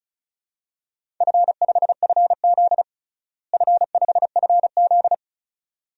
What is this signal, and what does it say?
Morse code, heard as a keyed tone, about 36 wpm, 700 Hz. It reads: F5FZ F5FZ
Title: DXCC-100 Call Signs - 4 Characters - Repeated 2x - 35wpm